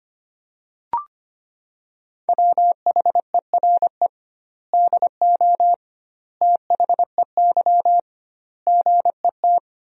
Morse code, heard as a keyed tone, about 25 wpm, 700 Hz. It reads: WHERE DO THEY GET
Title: 2x Word-Spacing - Sentences from Top 200 Words - 25wpm